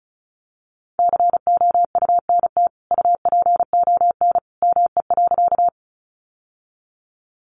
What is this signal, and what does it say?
Morse code, heard as a keyed tone, about 34 wpm, 700 Hz. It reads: COUNT UPON ME.
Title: Sentences from Top 1500 Words 35wpm